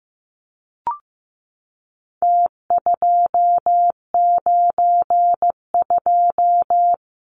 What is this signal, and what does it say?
Morse code, heard as a keyed tone, about 15 wpm, 700 Hz. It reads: T292
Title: Numbers - Contest Style - 4-Digit 15wpm